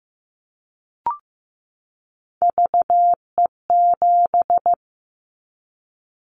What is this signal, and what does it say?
Morse code, heard as a keyed tone, about 15 wpm, 700 Hz. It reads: VE7
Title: Callsign Prefixes for Top 100 DXCC - 15wpm